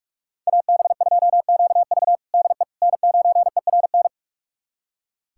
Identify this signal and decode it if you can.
Morse code, heard as a keyed tone, about 46 wpm, 700 Hz. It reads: AB2QV DE N9ERN
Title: USA Callsign DE Callsign - 5 Characters - 45wpm